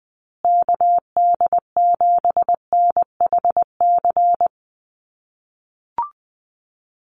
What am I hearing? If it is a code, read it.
Morse code, heard as a keyed tone, about 19 wpm, 700 Hz. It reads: KD7NHC
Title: USA Call Signs Repeated 3x 20wpm